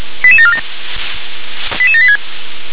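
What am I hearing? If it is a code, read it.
Five-tone selective call (ZVEI), heard as quick runs of five tones, tones of about 70 ms. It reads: 88747 88786
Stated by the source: ZVEI five-tone call 88170 received by SDR.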